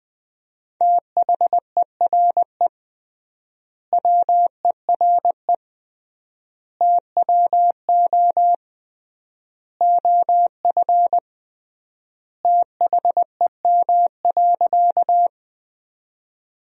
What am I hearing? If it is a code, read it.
Morse code, heard as a keyed tone, about 20 wpm, 700 Hz. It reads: THERE WERE TWO OF THEM.
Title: 3x Word-Spacing - Sentences from Top 100 Words - 20wpm